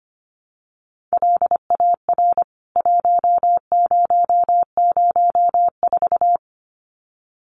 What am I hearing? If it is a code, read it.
Morse code, heard as a keyed tone, about 25 wpm, 700 Hz. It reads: LAR 1004
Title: CWT Exchange 25wpm